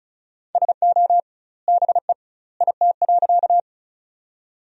Morse code, heard as a keyed tone, about 35 wpm, 700 Hz. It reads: SO BE IT.
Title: 2x Word-Spacing - Sentences from Top 100 Words - 35wpm